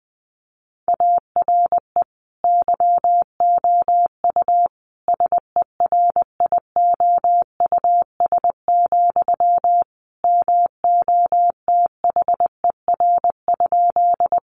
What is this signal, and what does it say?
Morse code, heard as a keyed tone, about 20 wpm, 700 Hz. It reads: ARE YOU SERIOUS, MOTHER?
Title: Sentences from Top 1100 Words 20wpm